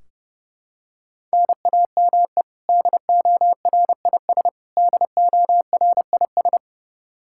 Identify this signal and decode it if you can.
Morse code, heard as a keyed tone, about 32 wpm, 700 Hz. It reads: NAME DORIS DORIS
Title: Practice QSO - Exchange 1 - Beginning - 30wpm